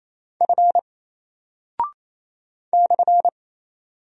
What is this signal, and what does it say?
Morse code, heard as a keyed tone, about 27 wpm, 700 Hz. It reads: F /
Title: Single Character (Letter:Number:Punctuation:Prosign) 28wpm